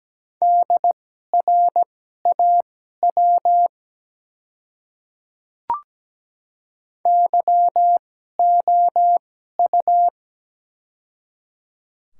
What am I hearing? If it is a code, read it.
Morse code, heard as a keyed tone, about 17 wpm, spaced as for 12 wpm, 700 Hz. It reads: DRAW YOU
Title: 2x Character Spacing - Top 300 Words - 17wpm